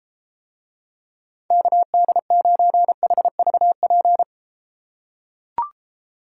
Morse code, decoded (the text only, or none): KD9HVP